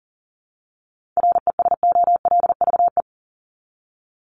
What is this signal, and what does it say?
Morse code, heard as a keyed tone, about 40 wpm, 700 Hz. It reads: RESOLVE
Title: Top 1751-2000 Words 40wpm